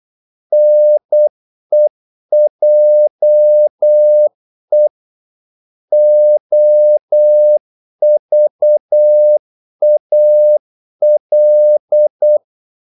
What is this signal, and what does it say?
Morse code, heard as a keyed tone, about 8 wpm, 600 Hz. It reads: NEJE OVAL
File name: Korte ord DK 8wpm